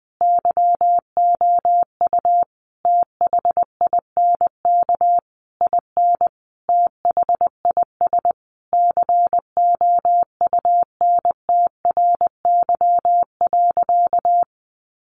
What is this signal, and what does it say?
Morse code, heard as a keyed tone, about 20 wpm, 700 Hz. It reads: YOU THINK IN THIS COUNTRY.
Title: Sentences from Top 1200 Words 20wpm